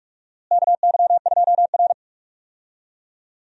Morse code, heard as a keyed tone, about 44 wpm, 700 Hz. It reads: KY2R
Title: USA Call Signs - 4 Characters - Repeated 3x - 45wpm